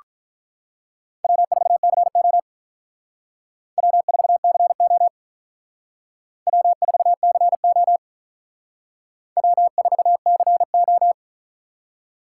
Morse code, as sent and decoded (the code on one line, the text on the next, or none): .-- ....- -.-. --- / .-- ....- -.-. --- / .-- ....- -.-. --- / .-- ....- -.-. ---
W4CO W4CO W4CO W4CO